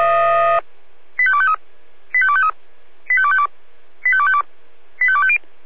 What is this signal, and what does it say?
Five-tone selective call (ZVEI), heard as quick runs of five tones, tones of about 70 ms. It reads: 85253 85253 85252 85252 85250